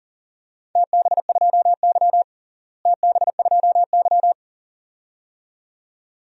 Morse code, heard as a keyed tone, about 40 wpm, 700 Hz. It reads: TB2Y TB2Y
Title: DXCC-100 Call Signs - 4 Characters - Repeated 2x - 40wpm